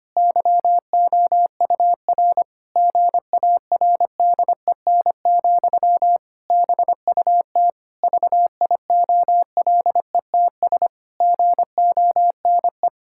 Morse code, heard as a keyed tone, about 25 wpm, 700 Hz. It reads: YOUR GARDEN, BUT VIOLETS GONE
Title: Pangram Challenge v2 - 25wpm